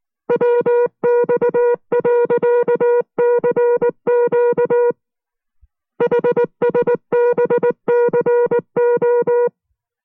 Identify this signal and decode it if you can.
Morse code, heard as a keyed tone, about 19 wpm, 480 Hz. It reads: WX.CQ HSBCO